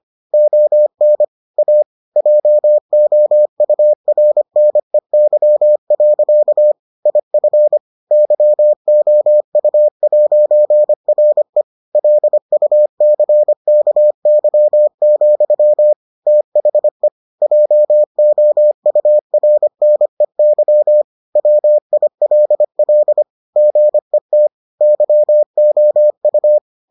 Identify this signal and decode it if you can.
Morse code, heard as a keyed tone, about 25 wpm, 600 Hz. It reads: ON A JOURNEY. IF YOU'RE LUCKY, THE JOURNEY WILL GET YOU